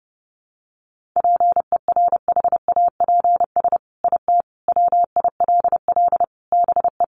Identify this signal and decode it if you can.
Morse code, heard as a keyed tone, about 30 wpm, 700 Hz. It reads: PERHAPS IT WILL BE
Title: Sentences from Top 1500 Words 30wpm